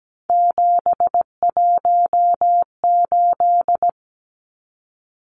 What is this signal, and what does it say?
Morse code, heard as a keyed tone, about 17 wpm, 700 Hz. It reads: 718
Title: Numbers - 3-Digit 17wpm